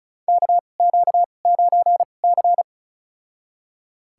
Morse code, heard as a keyed tone, about 36 wpm, spaced as for 25 wpm, 700 Hz. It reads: KQ9C
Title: USA Call Signs - 2x Character Spacing - 35wpm